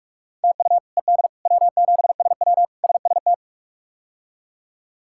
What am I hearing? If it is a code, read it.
Morse code, heard as a keyed tone, about 41 wpm, 700 Hz. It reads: TU ED W7SW SST
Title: K1USN SST QSO Practice - 45wpm